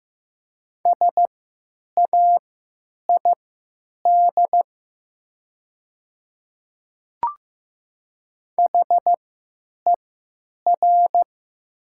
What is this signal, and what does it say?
Morse code, heard as a keyed tone, about 16 wpm, spaced as for 9 wpm, 700 Hz. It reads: SAID HER
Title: 3x Character Spacing - Top 300 Words - 15wpm